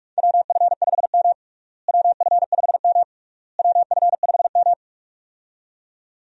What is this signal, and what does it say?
Morse code, heard as a keyed tone, about 46 wpm, 700 Hz. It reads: WF5M WF5M WF5M
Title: USA Call Signs - 4 Characters - Repeated 3x - 45wpm